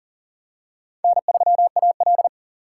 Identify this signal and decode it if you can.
Morse code, heard as a keyed tone, about 40 wpm, 700 Hz. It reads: N3AL
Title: USA Call Signs - 4 Characters - 3x Character Spacing - 40wpm